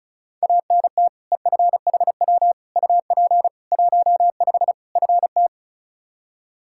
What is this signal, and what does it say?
Morse code, heard as a keyed tone, about 36 wpm, 700 Hz. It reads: ANT EFHW UP 15 FT
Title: Practice QSO - Exchange 2 - Info - 35wpm